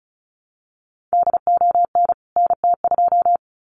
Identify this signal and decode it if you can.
Morse code, heard as a keyed tone, about 34 wpm, 700 Hz. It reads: DON NT2
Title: CWT Exchange 35wpm